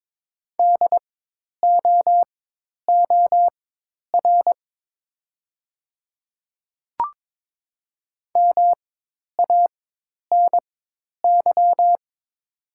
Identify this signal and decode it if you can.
Morse code, heard as a keyed tone, about 22 wpm, spaced as for 10 wpm, 700 Hz. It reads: DOOR MANY
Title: 4x Character Spacing - Top 300 Words - 22wpm